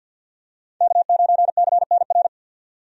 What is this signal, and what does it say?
Morse code, heard as a keyed tone, about 50 wpm, 700 Hz. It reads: K9CNR
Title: USA Call Signs - 5 Characters - 50wpm